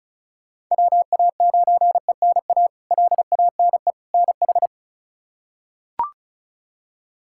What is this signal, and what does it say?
Morse code, heard as a keyed tone, about 36 wpm, 700 Hz. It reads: WA9ENA LANE NH
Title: K1USN SST QSO Practice - 35wpm